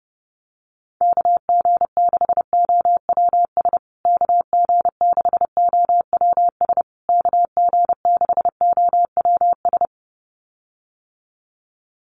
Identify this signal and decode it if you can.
Morse code, heard as a keyed tone, about 30 wpm, 700 Hz. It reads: KG6OWS KG6OWS KG6OWS
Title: USA Call Signs Repeated 3x 30wpm